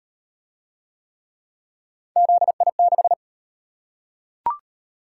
Morse code, heard as a keyed tone, about 34 wpm, 700 Hz. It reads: ZI6